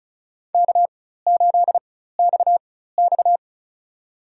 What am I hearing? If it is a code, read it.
Morse code, heard as a keyed tone, about 36 wpm, spaced as for 16 wpm, 700 Hz. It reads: K8XX
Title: USA Call Signs - 4 Characters - 4x Character Spacing - 35wpm